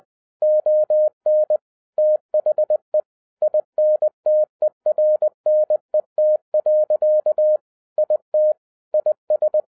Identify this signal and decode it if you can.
Morse code, heard as a keyed tone, about 20 wpm, 610 Hz. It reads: ON THE INTERNET. IT IS